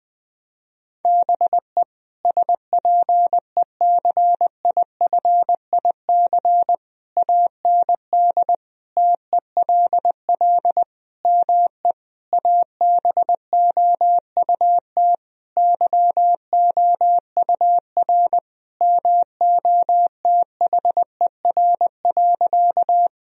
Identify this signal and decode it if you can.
Morse code, heard as a keyed tone, about 20 wpm, 700 Hz. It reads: BE SPECIFIC AND TELL ME ABOUT YOUR MOTHER.
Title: Sentences from Top 1000 Words - Review Part 3 - 20wpm